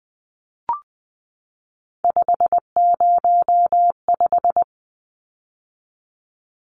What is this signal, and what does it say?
Morse code, heard as a keyed tone, about 20 wpm, 700 Hz. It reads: 505
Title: Numbers - 3-Digit 20wpm